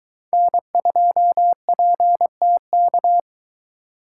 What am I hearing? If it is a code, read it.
Morse code, heard as a keyed tone, about 22 wpm, 700 Hz. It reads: N2PTK